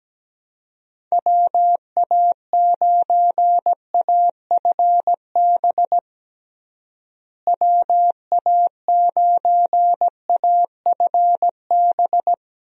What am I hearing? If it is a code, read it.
Morse code, heard as a keyed tone, about 17 wpm, 700 Hz. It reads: WA9AFB WA9AFB
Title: USA Call Signs - 6 Characters - Repeated 3x - 17wpm